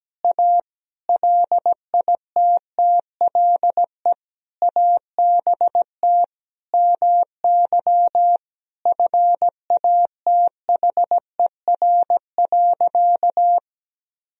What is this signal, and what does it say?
Morse code, heard as a keyed tone, about 17 wpm, 700 Hz. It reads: A LITTLE ABT MY FATHER.
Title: Sentences from Top 400 Words and Abbreviations 17wpm